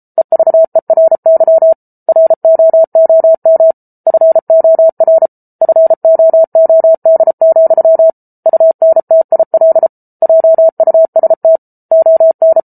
Morse code, heard as a keyed tone, about 34 wpm, 650 Hz. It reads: EVERY ROOM FOR FOOD, UNTIL JUST ON